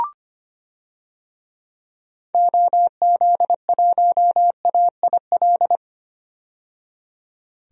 Morse code, heard as a keyed tone, about 25 wpm, 700 Hz. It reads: OZ1AIL
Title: DXCC-100 Call Signs - 6 Characters - 25wpm